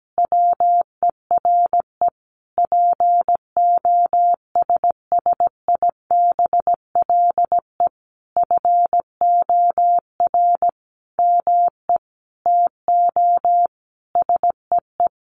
Morse code, heard as a keyed tone, about 17 wpm, 700 Hz. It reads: WERE POSSIBLE FOR ME TO SEE
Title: Sentences from Top 500 Words 17wpm